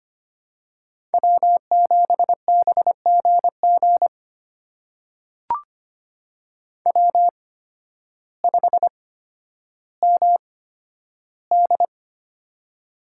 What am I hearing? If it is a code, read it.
Morse code, heard as a keyed tone, about 25 wpm, spaced as for 12 wpm, 700 Hz. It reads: W7BGG W5MD
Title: USA Call Signs - 8x Character Spacing - 25wpm